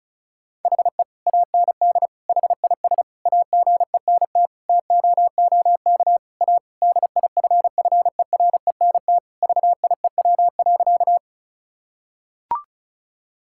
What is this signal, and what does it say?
Morse code, heard as a keyed tone, about 36 wpm, 700 Hz. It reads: HE AND HIS AGENT TOOK A DIFFERENT VIEW.
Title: Sentences from Top 1600 Words 35wpm